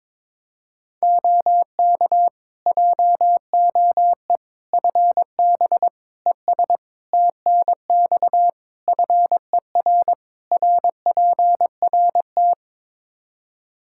Morse code, heard as a keyed tone, about 22 wpm, 700 Hz. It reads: OK JOE FB ES TNX FER RPRT
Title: K1USN SST QSO Practice 22wpm